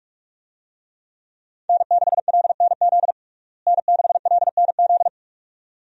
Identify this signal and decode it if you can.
Morse code, heard as a keyed tone, about 41 wpm, 700 Hz. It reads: N6LNZ N6LNZ
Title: USA Call Signs - 5 Characters - Repeated 2x - 45wpm